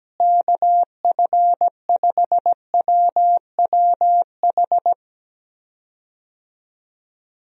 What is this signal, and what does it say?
Morse code, heard as a keyed tone, about 17 wpm, 700 Hz. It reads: KF5WWH
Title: USA Call Signs - 6 Characters - Repeated 2x - 17wpm